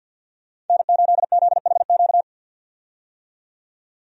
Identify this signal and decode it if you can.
Morse code, heard as a keyed tone, about 47 wpm, 700 Hz. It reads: N8ZHQ
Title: USA Call Signs - 5 Characters - 50wpm